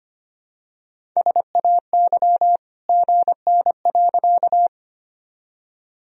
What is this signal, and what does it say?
Morse code, heard as a keyed tone, about 25 wpm, 700 Hz. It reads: SAY GN.
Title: Sentences from Top 300 Words and Abbreviations 25wpm